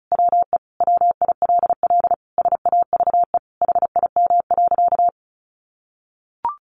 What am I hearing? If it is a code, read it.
Morse code, heard as a keyed tone, about 34 wpm, 700 Hz. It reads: WE WILL SAVE HIM.
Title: Sentences from Top 1100 Words 35wpm